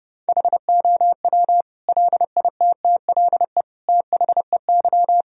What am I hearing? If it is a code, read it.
Morse code, heard as a keyed tone, about 30 wpm, 700 Hz. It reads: HOW LITTLE THEY
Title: Sentences from Top 1000 Words - Review Part 1 - 30wpm